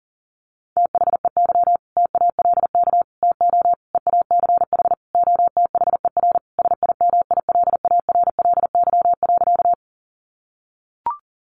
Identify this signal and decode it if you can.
Morse code, heard as a keyed tone, about 40 wpm, 700 Hz. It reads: THEY TALK TO EACH OTHER SIMILARLY.
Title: Sentences from Top 1400 Words 40wpm